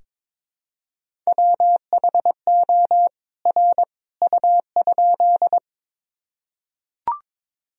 Morse code, heard as a keyed tone, about 22 wpm, 700 Hz. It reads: WHO R U?
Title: Sentences from Top 100 Words and Abbreviations 22wpm